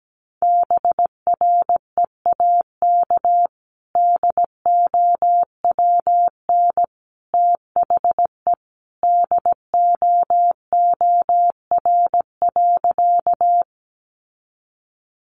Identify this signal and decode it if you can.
Morse code, heard as a keyed tone, about 17 wpm, 700 Hz. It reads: BREAK DOWN THE DOOR.
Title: Sentences from Top 900 Words 17wpm